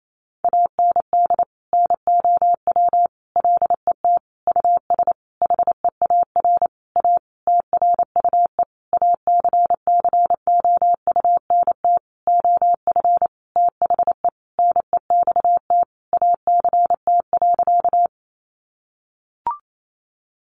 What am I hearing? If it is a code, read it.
Morse code, heard as a keyed tone, about 28 wpm, 700 Hz. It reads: AND NOW LET US HEAR A TRUE ACCOUNT OF THE NEXT ACT.
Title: Sentences from Top 900 Words 28wpm